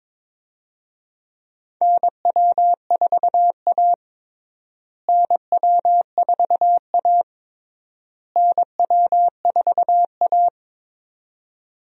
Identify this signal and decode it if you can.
Morse code, heard as a keyed tone, about 22 wpm, 700 Hz. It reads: NW4A NW4A NW4A
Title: USA Call Signs - 4 Characters - Repeated 3x - 22wpm